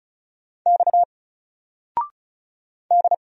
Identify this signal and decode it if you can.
Morse code, heard as a keyed tone, about 36 wpm, 700 Hz. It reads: X D
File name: Single-Letter-Number-35wpm